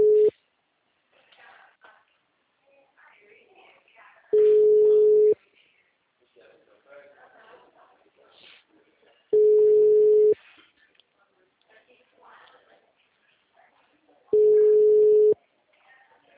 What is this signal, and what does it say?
Ringback tone (Europe, 425 Hz), with speech.